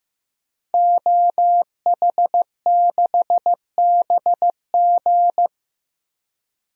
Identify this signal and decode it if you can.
Morse code, heard as a keyed tone, about 15 wpm, 700 Hz. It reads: OH6BG
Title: DXCC-100 Call Signs - 5 Characters - Repeated 2x - 15wpm